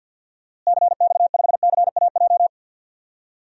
Morse code, heard as a keyed tone, about 47 wpm, 700 Hz. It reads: CX5CAJ
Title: DXCC-100 Call Signs - 6 Characters - Repeated 2x - 50wpm